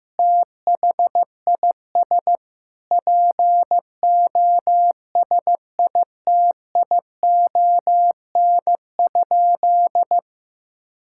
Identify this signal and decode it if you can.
Morse code, heard as a keyed tone, about 15 wpm, 700 Hz. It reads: THIS POSITION?
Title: Sentences from Top 1000 Words - Review Part 5 - 15wpm